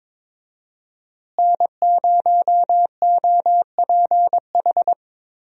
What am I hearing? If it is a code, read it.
Morse code, heard as a keyed tone, about 22 wpm, 700 Hz. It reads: N0OPH
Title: USA Call Signs - 5 Characters - Repeated 3x - 22wpm